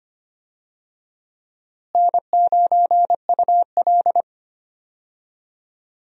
Morse code, heard as a keyed tone, about 25 wpm, 700 Hz. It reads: N9UL
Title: USA Call Signs - 4 Characters - Repeated 3x - 25wpm